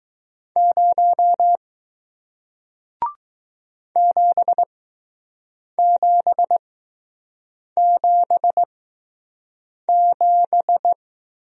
Morse code, heard as a keyed tone, about 20 wpm, spaced as for 6 wpm, 700 Hz. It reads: 0 7777